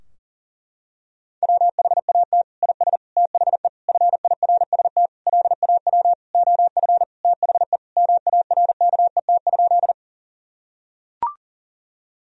Morse code, heard as a keyed tone, about 37 wpm, 700 Hz. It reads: WHAT IS THE FIRST LAW OF THE MARKET?
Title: Sentences from Top 1000 Words - Review Part 3 - 40wpm